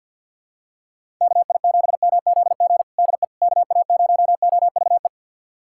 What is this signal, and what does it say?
Morse code, heard as a keyed tone, about 50 wpm, 700 Hz. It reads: KI7MZG DE KA0OVE
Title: USA Callsign DE Callsign 50wpm